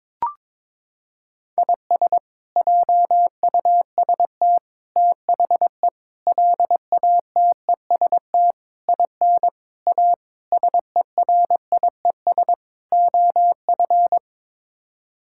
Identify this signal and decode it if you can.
Morse code, heard as a keyed tone, about 21 wpm, 700 Hz. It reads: IS JUST THE LATEST IN A SERIES OF
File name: practice / 22wpm / Sets of 8 Words - 22wpm